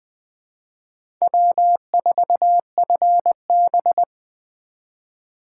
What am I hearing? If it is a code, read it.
Morse code, heard as a keyed tone, about 20 wpm, 700 Hz. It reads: W4FB
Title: USA Call Signs - 4 Characters - 4x Character Spacing - 20wpm